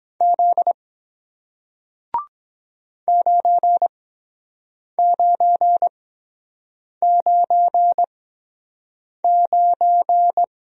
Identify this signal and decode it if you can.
Morse code, heard as a keyed tone, about 24 wpm, spaced as for 7 wpm, 700 Hz. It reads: Z 9999